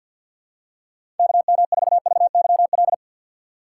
Morse code, heard as a keyed tone, about 52 wpm, 700 Hz. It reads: KM4VYL
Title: USA Call Signs - 6 Characters - 8x Character Spacing - 50wpm